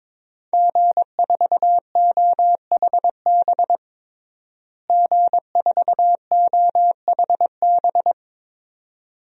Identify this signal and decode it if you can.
Morse code, heard as a keyed tone, about 23 wpm, 700 Hz. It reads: G4OHB G4OHB
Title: DXCC-100 Call Signs - 5 Characters - Repeated 2x - 22wpm